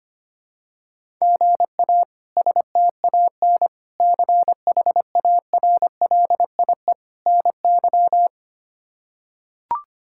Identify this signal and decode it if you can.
Morse code, heard as a keyed tone, about 25 wpm, 700 Hz. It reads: GA STAN CHARLIE NY
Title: K1USN SST QSO Practice - 25wpm